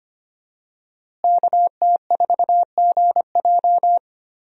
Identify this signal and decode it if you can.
Morse code, heard as a keyed tone, about 25 wpm, 700 Hz. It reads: KT4GJ
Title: USA Call Signs - 5 Characters - Repeated 2x - 25wpm